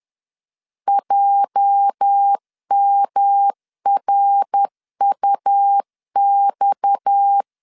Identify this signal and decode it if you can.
Morse code, heard as a keyed tone, about 10 wpm, 800 Hz. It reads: JMRUX
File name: LEZIONI / 8 YJ45 / 8 YJ 10wpm 2651